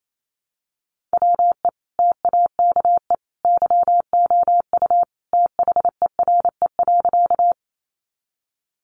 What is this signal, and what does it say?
Morse code, heard as a keyed tone, about 28 wpm, 700 Hz. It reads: WE TAKE YOU THERE.
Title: A Sound Of Thunder 28wpm